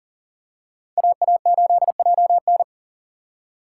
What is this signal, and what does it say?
Morse code, heard as a keyed tone, about 40 wpm, 700 Hz. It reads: AA8JN
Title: USA Call Signs - 5 Characters - 3x Character Spacing - 40wpm